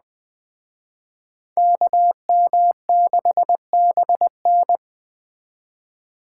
Morse code, heard as a keyed tone, about 21 wpm, 700 Hz. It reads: KM6BN